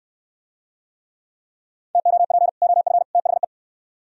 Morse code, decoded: TOP OF THE

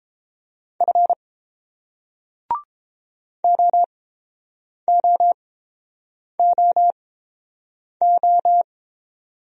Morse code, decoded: F OOOO

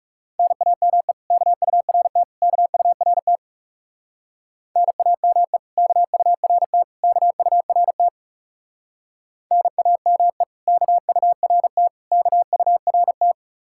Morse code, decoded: NAME KURT KURT NAME KURT KURT NAME KURT KURT

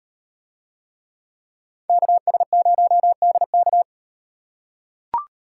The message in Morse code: KS0DK

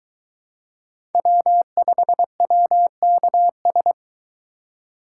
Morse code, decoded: W5WKS

